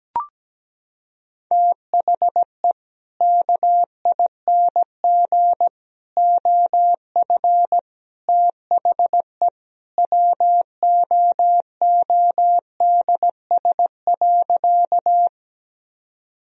Morse code, heard as a keyed tone, about 17 wpm, 700 Hz. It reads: THE KING OF THE WOODS.
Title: Sentences from Top 1000 Words - Review Part 2 - 17wpm